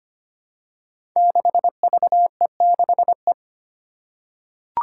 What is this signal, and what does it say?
Morse code, heard as a keyed tone, about 25 wpm, 700 Hz. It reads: 6VE6E
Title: Five Challenging Letter-Number Combinations 25wpm